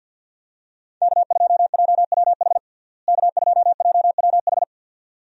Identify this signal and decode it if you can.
Morse code, heard as a keyed tone, about 50 wpm, 700 Hz. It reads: K2JWH K2JWH
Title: USA Call Signs - 5 Characters - Repeated 2x - 50wpm